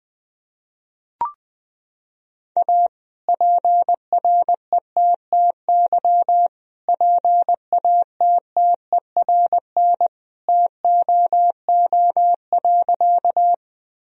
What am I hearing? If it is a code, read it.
Morse code, heard as a keyed tone, about 21 wpm, 700 Hz. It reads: A PRETTY PATTERN TOO.
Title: Sentences from Top 1000 Words - Review Part 2 - 20wpm